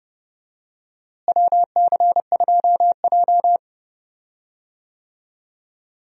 Morse code, heard as a keyed tone, about 30 wpm, 700 Hz. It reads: WC2J
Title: USA Call Signs - 4 Characters - Repeated 3x - 30wpm